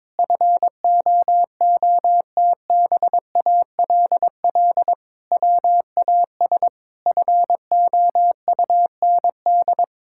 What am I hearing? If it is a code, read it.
Morse code, heard as a keyed tone, about 22 wpm, 700 Hz. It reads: FOOTBALL WAS FOUND
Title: Sentences from Top 1600 Words 22wpm